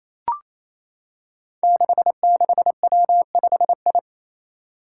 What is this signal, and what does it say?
Morse code, heard as a keyed tone, about 28 wpm, 700 Hz. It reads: 66W5I
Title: Five Challenging Letter-Number Combinations 28wpm